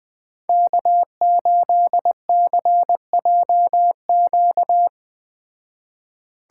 Morse code, heard as a keyed tone, about 19 wpm, 700 Hz. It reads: K8CJQ
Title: Speed Racing - USA Call Signs 17wpm